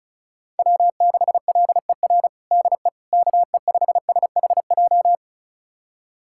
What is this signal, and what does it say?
Morse code, heard as a keyed tone, about 36 wpm, 700 Hz. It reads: W6LER DE KE5SHJ
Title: USA Callsign DE Callsign 35wpm